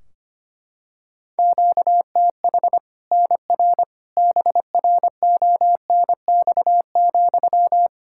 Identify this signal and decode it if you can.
Morse code, heard as a keyed tone, about 25 wpm, 700 Hz. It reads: QTH NR BRONX,
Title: Practice QSO - Exchange 1 - Beginning - 25wpm